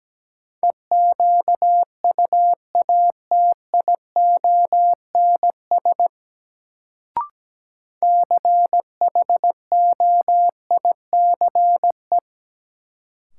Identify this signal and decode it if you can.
Morse code, heard as a keyed tone, about 17 wpm, 700 Hz. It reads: EQUATIONS CHOICE
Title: Sets of 5 Words - New Word Drill - 17wpm